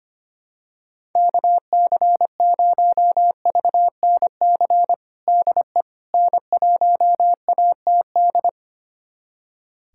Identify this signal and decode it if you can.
Morse code, heard as a keyed tone, about 25 wpm, 700 Hz. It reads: KC0VNC DE N1ATD
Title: USA Callsign DE Callsign 25wpm